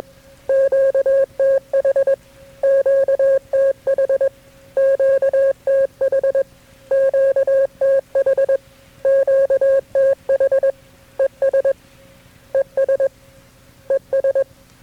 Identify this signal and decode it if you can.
Morse code, heard as a keyed tone, about 20 wpm, 540 Hz. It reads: QTH QTH QTH QTH QTH ES ES ES
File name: KB6MT-tape3-side1